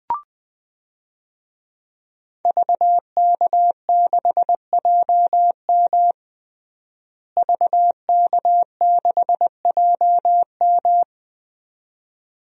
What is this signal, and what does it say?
Morse code, heard as a keyed tone, about 20 wpm, 700 Hz. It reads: VK6JM VK6JM
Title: DXCC-100 Call Signs - 5 Characters - Repeated 2x - 20wpm